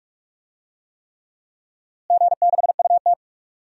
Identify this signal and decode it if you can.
Morse code, heard as a keyed tone, about 46 wpm, 700 Hz. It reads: G6UT
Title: DXCC-100 Call Signs - 4 Characters - Repeated 3x - 45wpm